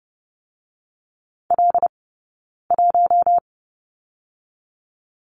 Morse code, heard as a keyed tone, about 30 wpm, 700 Hz. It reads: L 1
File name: Lesson 15d - Number 5 - Two Characters 30wpm